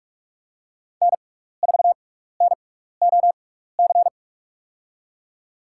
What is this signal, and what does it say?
Morse code, heard as a keyed tone, about 47 wpm, spaced as for 16 wpm, 700 Hz. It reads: N4NOC
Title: USA Call Signs - 5 Characters - 6x Character Spacing - 45wpm